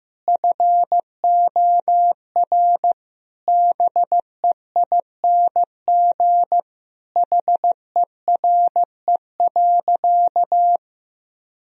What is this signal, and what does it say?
Morse code, heard as a keyed tone, about 15 wpm, 700 Hz. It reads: FOR BEING HERE.
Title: Sentences from Top 1000 Words - Review Part 2 - 15wpm